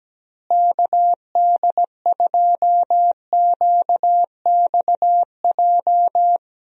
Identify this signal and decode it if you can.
Morse code, heard as a keyed tone, about 17 wpm, 700 Hz. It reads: KD2QXJ
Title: USA Call Signs - 6 Characters - Repeated 2x - 17wpm